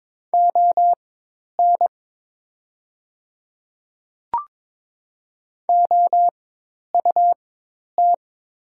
Morse code, heard as a keyed tone, about 22 wpm, spaced as for 10 wpm, 700 Hz. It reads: ON OUT
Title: 4x Character Spacing - Top 100 Words - 22wpm